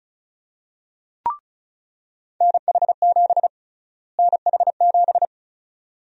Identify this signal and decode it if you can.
Morse code, heard as a keyed tone, about 36 wpm, 700 Hz. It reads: NH7 NH7
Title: USA Call Sign Prefixes - Repeated 2x - 35wpm